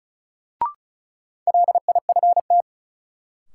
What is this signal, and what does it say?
Morse code, heard as a keyed tone, about 36 wpm, 700 Hz. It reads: LIFT
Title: Sentences from Top 1000 Words - New Word Drill - 35wpm